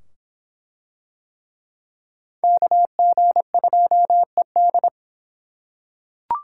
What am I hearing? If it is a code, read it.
Morse code, heard as a keyed tone, about 24 wpm, 700 Hz. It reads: KG2ED